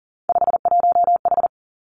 Morse code, heard as a keyed tone, about 41 wpm, 700 Hz. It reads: 51H
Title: Three Challenging Letter-Number Combinations 40wpm